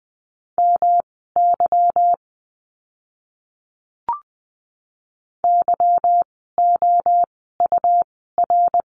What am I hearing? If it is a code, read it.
Morse code, heard as a keyed tone, about 20 wpm, spaced as for 15 wpm, 700 Hz. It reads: MY YOUR
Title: 237 Lesson 38f - Letter q - Words 20wpm